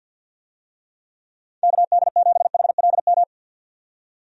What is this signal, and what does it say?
Morse code, heard as a keyed tone, about 53 wpm, 700 Hz. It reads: KD7HLM
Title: USA Call Signs - 6 Characters - 50wpm